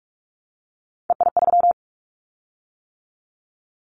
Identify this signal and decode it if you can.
Morse code, heard as a keyed tone, about 46 wpm, 700 Hz. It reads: EI3
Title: Callsign Prefixes for Top 50 DXCC - 45wpm